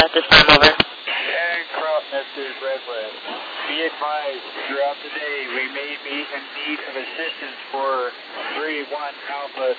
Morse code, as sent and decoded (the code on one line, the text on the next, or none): none